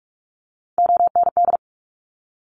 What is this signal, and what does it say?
Morse code, heard as a keyed tone, about 46 wpm, 700 Hz. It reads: OND
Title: Top Three Letter Combinations in Words 45wpm